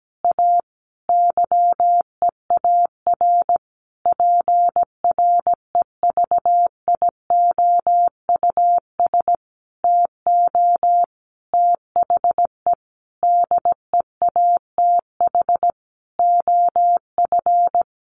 Morse code, heard as a keyed tone, about 17 wpm, 700 Hz. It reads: A YEAR PREVIOUS TO THE DEATH OF
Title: Sentences from Top 1500 Words 17wpm